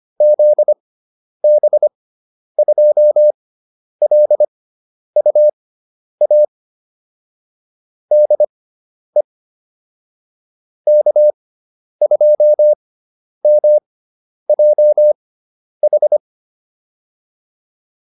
Morse code, heard as a keyed tone, about 25 wpm, spaced as for 10 wpm, 600 Hz. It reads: ZB2LUA DE K2MJH
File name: QSO402 10